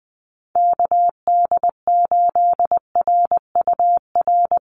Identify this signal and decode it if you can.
Morse code, heard as a keyed tone, about 20 wpm, 700 Hz. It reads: KD8RUR
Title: USA Call Signs 20wpm